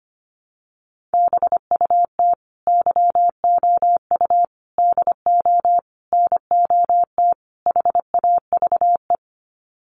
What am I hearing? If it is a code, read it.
Morse code, heard as a keyed tone, about 25 wpm, 700 Hz. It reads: BUT YOU DO NOT HAVE
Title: Sets of 5 Words Encore 25wpm